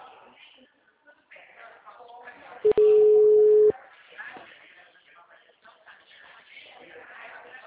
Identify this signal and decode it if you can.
Ringback tone (Europe, 425 Hz), with speech.
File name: out- -2031- - - .49286